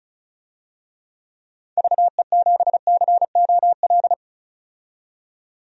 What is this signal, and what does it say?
Morse code, heard as a keyed tone, about 36 wpm, 700 Hz. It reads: VE7COL
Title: DXCC-100 Call Signs - 6 Characters - 35wpm